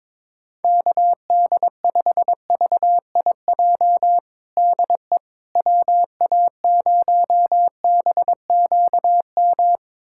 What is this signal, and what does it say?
Morse code, heard as a keyed tone, about 23 wpm, 700 Hz. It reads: KD5VIJ DE WA0BQM
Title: USA Callsign DE Callsign - 6 Characters - 22wpm